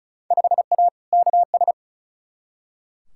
Morse code, heard as a keyed tone, about 36 wpm, 690 Hz. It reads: 5A KS
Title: Field Day Exchanges Encore - 35wpm